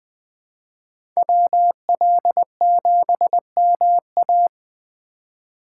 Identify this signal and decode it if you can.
Morse code, heard as a keyed tone, about 20 wpm, 700 Hz. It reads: WL7MA
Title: USA Call Signs - 5 Characters - 4x Character Spacing - 20wpm